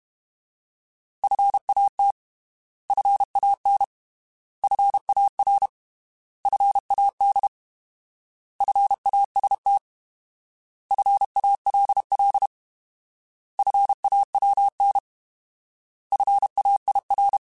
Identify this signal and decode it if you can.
Morse code, heard as a keyed tone, about 32 wpm, 800 Hz. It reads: FAT FAN FAR FAD FAST FALL FAWN FAIR